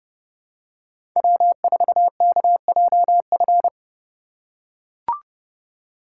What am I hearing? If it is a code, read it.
Morse code, heard as a keyed tone, about 30 wpm, 700 Hz. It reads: W4KJF